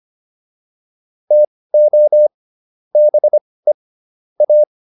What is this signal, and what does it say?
Morse code, heard as a keyed tone, about 25 wpm, spaced as for 18 wpm, 600 Hz. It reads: TO BE A